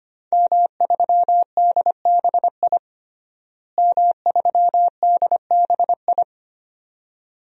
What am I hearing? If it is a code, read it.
Morse code, heard as a keyed tone, about 25 wpm, 700 Hz. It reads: M3DBI M3DBI
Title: DXCC-100 Call Signs - 5 Characters - Repeated 2x - 25wpm